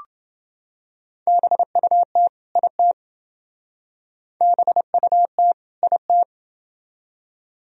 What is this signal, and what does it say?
Morse code, heard as a keyed tone, about 27 wpm, 700 Hz. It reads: BUT IT BUT IT